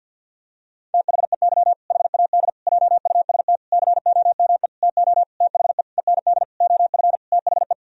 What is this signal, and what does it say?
Morse code, heard as a keyed tone, about 50 wpm, 700 Hz. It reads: THEY HAD JUST COME TO THE END OF THE
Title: Sentences from Top 1000 Words - Review Part 2 - 50wpm